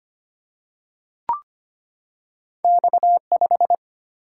Morse code, heard as a keyed tone, about 25 wpm, 700 Hz. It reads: X5
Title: Single-Letter Single-Number 25wpm